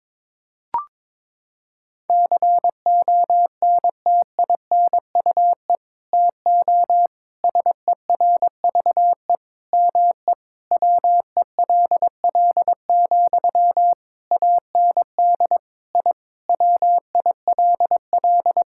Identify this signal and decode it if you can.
Morse code, heard as a keyed tone, about 22 wpm, 700 Hz. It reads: CONTINUE TO SERVE ME WELL, AND I WILL
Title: Sentences from Top 1000 Words - Review Part 6 - 22wpm